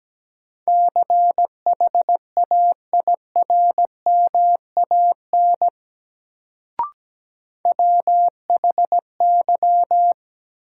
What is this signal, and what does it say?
Morse code, heard as a keyed tone, about 17 wpm, 700 Hz. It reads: CHAIRMAN WHY
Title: Sets of 3 Words - New Word Drill - 17wpm